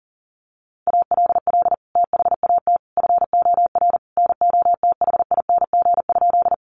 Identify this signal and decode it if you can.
Morse code, heard as a keyed tone, about 40 wpm, 700 Hz. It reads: ALL THAT FOR NOTHING?
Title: Sentences from Top 500 Words 40wpm